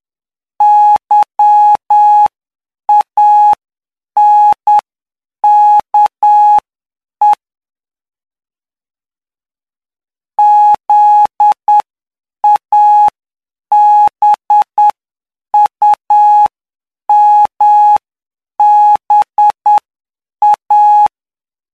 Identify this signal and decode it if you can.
Morse code, heard as a keyed tone, about 9 wpm, spaced as for 7 wpm, 840 Hz. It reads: YANKE ZABUMBA
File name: cw / py2lcd / 12-PY2LCD-Li��o 21